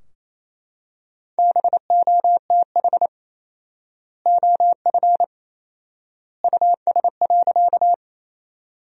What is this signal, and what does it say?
Morse code, heard as a keyed tone, about 28 wpm, 700 Hz. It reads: BOTH OF US.
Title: 4x Word-Spacing - Sentences from Top 300 Words - 28wpm